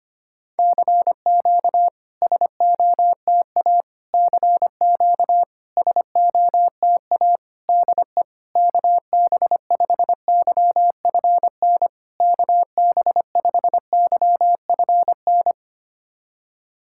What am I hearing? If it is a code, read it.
Morse code, heard as a keyed tone, about 25 wpm, 690 Hz. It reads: CQ SOTA CQ SOTA DE KB5YFN KB5YFN
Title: SOTA QSO Practice 25wpm